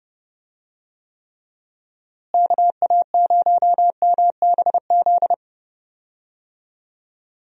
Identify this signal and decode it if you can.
Morse code, heard as a keyed tone, about 30 wpm, 700 Hz. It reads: KA0MBZ
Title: USA Call Signs - 6 Characters - Repeated 2x - 30wpm